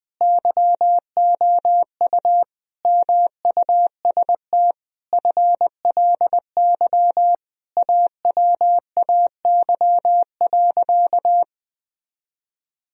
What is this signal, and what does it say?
Morse code, heard as a keyed tone, about 20 wpm, 700 Hz. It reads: YOU MUST FLY AWAY.
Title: Sentences from Top 1000 Words - Review Part 2 - 20wpm